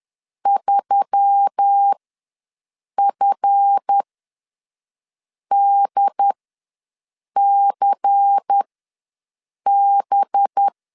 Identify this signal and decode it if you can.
Morse code, heard as a keyed tone, about 10 wpm, spaced as for 5 wpm, 800 Hz. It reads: 3FDCB
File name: LEZIONI / 8 YJ45 / 8 YJ ws 10wpm 4043